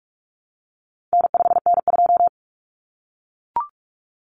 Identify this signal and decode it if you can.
Morse code, heard as a keyed tone, about 44 wpm, 700 Hz. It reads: N5N2
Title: Common Cut Numbers - 4-Digit 45wpm